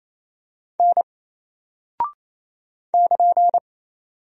Morse code, heard as a keyed tone, about 27 wpm, 700 Hz. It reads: N <KN>
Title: Single Character (Letter:Number:Punctuation:Prosign) 28wpm